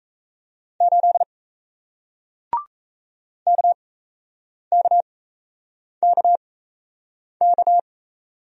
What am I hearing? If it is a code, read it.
Morse code, heard as a keyed tone, about 38 wpm, spaced as for 8 wpm, 700 Hz. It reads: 8 KKKK